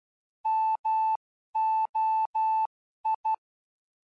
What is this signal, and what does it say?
Morse code, heard as a keyed tone, about 12 wpm, spaced as for 11 wpm, 880 Hz. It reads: MOI
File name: ardf.quickstart / moi